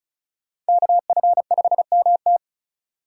Morse code, heard as a keyed tone, about 34 wpm, 700 Hz. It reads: KF5MT
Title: USA Call Signs - 5 Characters - 4x Character Spacing - 35wpm